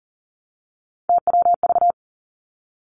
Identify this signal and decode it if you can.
Morse code, heard as a keyed tone, about 42 wpm, 700 Hz. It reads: TWV
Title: Three Challenging Letter-Number Combinations 40wpm